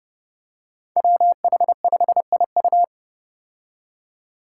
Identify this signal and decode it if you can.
Morse code, heard as a keyed tone, about 30 wpm, 700 Hz. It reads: WH5IU
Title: Five Challenging Letter-Number Combinations 30wpm